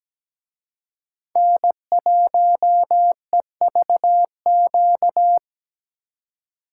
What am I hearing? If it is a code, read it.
Morse code, heard as a keyed tone, about 20 wpm, 700 Hz. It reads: N1EVQ
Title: Speed Racing - USA 1x3 Call Signs 17wpm